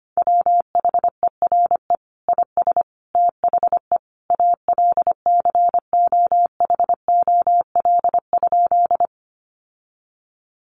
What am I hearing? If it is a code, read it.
Morse code, heard as a keyed tone, about 25 wpm, 700 Hz. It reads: WHERE IS THE ALCOHOL?
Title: Sentences from Top 1400 Words 25wpm